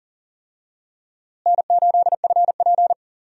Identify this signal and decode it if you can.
Morse code, heard as a keyed tone, about 40 wpm, 700 Hz. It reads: N8FP
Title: USA Call Signs - 4 Characters - Repeated 3x - 40wpm